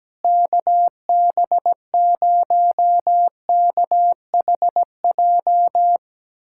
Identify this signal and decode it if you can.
Morse code, heard as a keyed tone, about 17 wpm, 700 Hz. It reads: KB0KHJ
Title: USA Callsign DE Callsign - 6 Characters - 17wpm